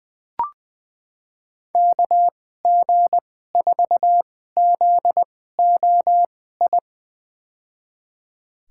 Morse code, heard as a keyed tone, about 20 wpm, spaced as for 15 wpm, 700 Hz. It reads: KG4ZOI
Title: USA Call Signs - 6 Characters - 2x Character Spacing - 20wpm